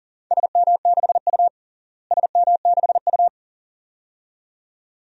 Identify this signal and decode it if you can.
Morse code, heard as a keyed tone, about 40 wpm, 700 Hz. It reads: SM6U SM6U
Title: DXCC-100 Call Signs - 4 Characters - Repeated 2x - 40wpm